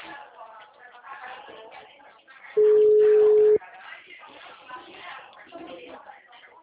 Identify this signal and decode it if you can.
Ringback tone (Europe, 425 Hz), with speech.